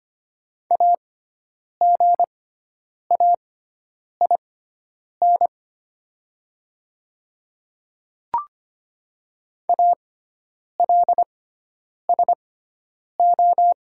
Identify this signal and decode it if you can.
Morse code, heard as a keyed tone, about 25 wpm, spaced as for 9 wpm, 700 Hz. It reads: AGAIN ALSO
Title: 6x Character Spacing - Top 500 Words - 25wpm